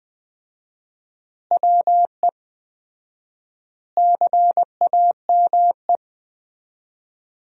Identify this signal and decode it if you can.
Morse code, heard as a keyed tone, about 20 wpm, 700 Hz. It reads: WE CAME
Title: 4x Word-Spacing - Sentences from Top 400 Words - 20wpm